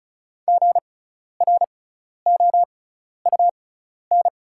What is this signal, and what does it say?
Morse code, heard as a keyed tone, about 36 wpm, spaced as for 12 wpm, 690 Hz. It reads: GROUN
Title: 6x Character Spacing - Top 500 Words - 35wpm